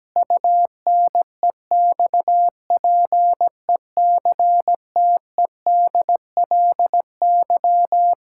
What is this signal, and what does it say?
Morse code, heard as a keyed tone, about 17 wpm, 700 Hz. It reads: UNEXPECTEDLY
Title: Pangram Challenge v2 - 17wpm